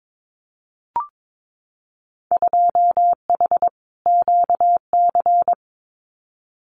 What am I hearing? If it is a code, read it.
Morse code, heard as a keyed tone, about 22 wpm, 700 Hz. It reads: 2H QC
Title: Winter Field Day Exchanges 22wpm